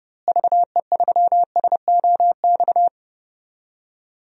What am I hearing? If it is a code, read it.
Morse code, heard as a keyed tone, about 30 wpm, 700 Hz. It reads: VE3SOX
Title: DXCC-100 Call Signs - 6 Characters - Repeated 3x - 30wpm